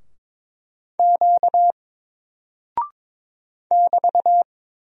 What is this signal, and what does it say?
Morse code, heard as a keyed tone, about 22 wpm, 700 Hz. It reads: Q <BT>
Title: Single Character (Letter:Number:Punctuation:Prosign) 22wpm